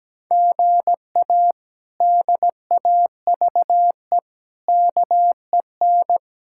Morse code, heard as a keyed tone, about 17 wpm, 700 Hz. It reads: GA DAVE KEN
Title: K1USN SST QSO Practice - 17wpm